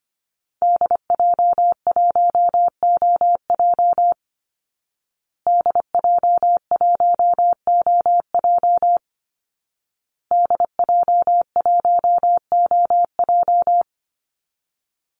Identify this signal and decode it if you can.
Morse code, heard as a keyed tone, about 25 wpm, 700 Hz. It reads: DJ1OJ DJ1OJ DJ1OJ
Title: Call Signs Repeated 2x - 25wpm